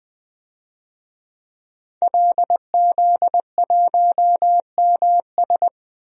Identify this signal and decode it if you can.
Morse code, heard as a keyed tone, about 20 wpm, 700 Hz. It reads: LZ1MS
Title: DXCC-100 Call Signs - 5 Characters - 20wpm